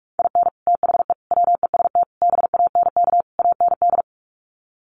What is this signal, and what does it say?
Morse code, heard as a keyed tone, about 44 wpm, 700 Hz. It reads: IN THE WEST BANK AND
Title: Sets of 5 Words Encore2 45wpm